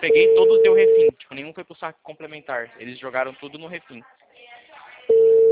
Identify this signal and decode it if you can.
Ringback tone (China, 450 Hz), with speech.
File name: out- -2022- - - .90333